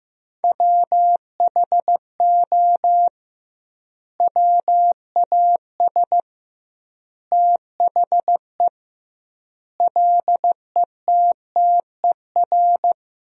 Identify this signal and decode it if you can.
Morse code, heard as a keyed tone, about 15 wpm, 700 Hz. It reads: WHO WAS THE LETTER
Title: 2x Word-Spacing - Sentences from Top 300 Words - 15wpm